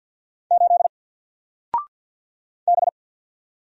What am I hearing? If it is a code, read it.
Morse code, heard as a keyed tone, about 49 wpm, 700 Hz. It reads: 8 B
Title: Single Letter-Number 50wpm